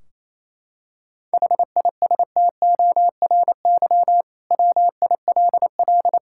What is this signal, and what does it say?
Morse code, heard as a keyed tone, about 28 wpm, 700 Hz. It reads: HISTORY WILL
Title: Sentences from Top 1600 Words 28wpm